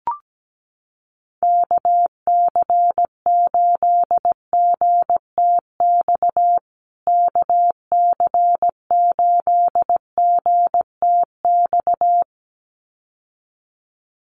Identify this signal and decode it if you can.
Morse code, heard as a keyed tone, about 17 wpm, 700 Hz. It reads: KC8GTX KC8GTX
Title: USA Call Signs Repeated 2x 17wpm